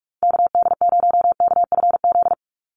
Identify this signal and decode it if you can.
Morse code, heard as a keyed tone, about 44 wpm, 700 Hz. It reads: KD0KFZ
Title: USA Call Signs 45wpm